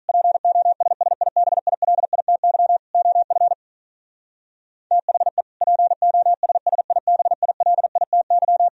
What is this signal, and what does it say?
Morse code, heard as a keyed tone, about 42 wpm, 700 Hz. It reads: POSSIBILITY OF THE POSSIBILITY
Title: Speed-Racing - Sets of 3 Words 35wpm